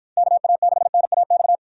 Morse code, heard as a keyed tone, about 53 wpm, 700 Hz. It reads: KA6NAX